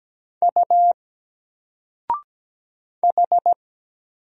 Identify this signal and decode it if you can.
Morse code, heard as a keyed tone, about 17 wpm, 700 Hz. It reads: U H
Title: Single Letter-Number 17wpm